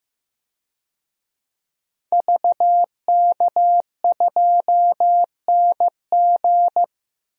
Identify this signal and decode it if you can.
Morse code, heard as a keyed tone, about 15 wpm, 700 Hz. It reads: VK2NG
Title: DXCC-100 Call Signs - 5 Characters - 15wpm